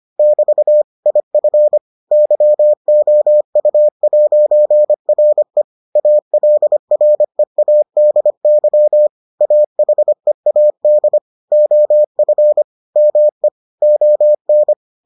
Morse code, heard as a keyed tone, about 25 wpm, 600 Hz. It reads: <BT> IF YOU'RE ALREADY AHEAD OF ME ON